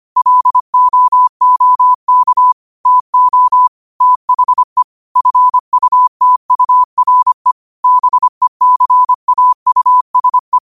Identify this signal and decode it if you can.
Morse code, heard as a keyed tone, about 25 wpm, 1000 Hz. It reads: LOOK TO THE FUTURE BECAUSE
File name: qotdfiles / 25wpm